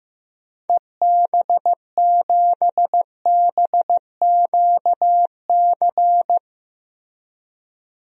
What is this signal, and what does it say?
Morse code, heard as a keyed tone, about 15 wpm, 700 Hz. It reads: EB7BQC
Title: DXCC-100 Call Signs - 6 Characters - Repeated 2x - 15wpm